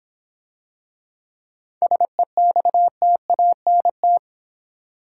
Morse code, heard as a keyed tone, about 26 wpm, 700 Hz. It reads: SEXTANT